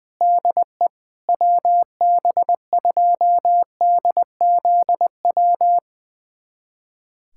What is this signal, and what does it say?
Morse code, heard as a keyed tone, about 20 wpm, 700 Hz. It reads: DE WB2DZW
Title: USA Callsign DE Callsign - 6 Characters - 20wpm